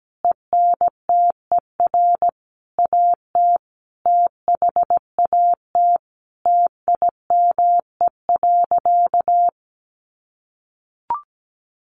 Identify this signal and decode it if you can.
Morse code, heard as a keyed tone, about 17 wpm, 700 Hz. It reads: ENTER AT THAT TIME.
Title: Sentences from Top 1300 Words 17wpm